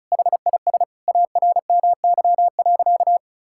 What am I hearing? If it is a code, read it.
Morse code, heard as a keyed tone, about 34 wpm, 700 Hz. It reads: HIS ARMY.
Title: Sentences from Top 1000 Words - Review Part 6 - 35wpm